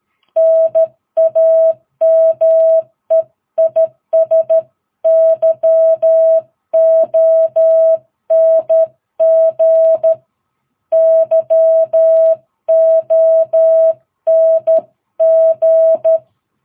Morse code, heard as a keyed tone, about 13 wpm, 660 Hz. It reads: NAME IS YONG YONG
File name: jh8cbh / 2022cw